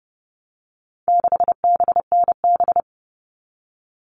Morse code, heard as a keyed tone, about 30 wpm, 700 Hz. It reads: 6BNB
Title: All Cut Numbers - 4-Digit 30wpm